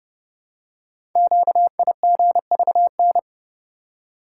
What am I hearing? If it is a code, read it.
Morse code, heard as a keyed tone, about 30 wpm, 700 Hz. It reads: QIGVN
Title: Code Groups - 5 Letters - 30wpm